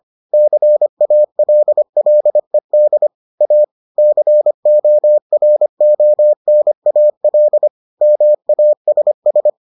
Morse code, heard as a keyed tone, about 25 wpm, 600 Hz. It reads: CALLED A CORONAL MASS